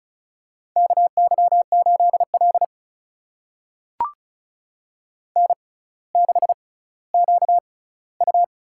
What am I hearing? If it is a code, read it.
Morse code, heard as a keyed tone, about 36 wpm, spaced as for 18 wpm, 700 Hz. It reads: KY8L N6QU
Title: USA Call Signs - 4 Characters - 6x Character Spacing - 35wpm